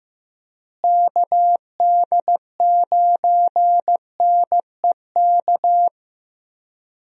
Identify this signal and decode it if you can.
Morse code, heard as a keyed tone, about 15 wpm, 700 Hz. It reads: KD9NEK
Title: USA Call Signs - 6 Characters - Repeated 3x - 15wpm